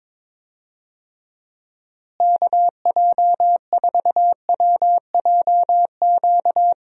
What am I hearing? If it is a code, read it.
Morse code, heard as a keyed tone, about 22 wpm, 700 Hz. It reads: KJ4WJQ
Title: USA Call Signs - 6 Characters - Repeated 2x - 22wpm